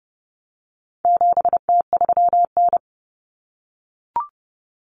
Morse code, heard as a keyed tone, about 30 wpm, 700 Hz. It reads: 7T3N
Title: Common Cut Numbers - 4-Digit 30wpm